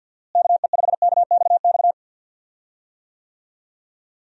Morse code, heard as a keyed tone, about 50 wpm, 700 Hz. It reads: KE5KXX
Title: USA Call Signs - 6 Characters - Repeated 2x - 50wpm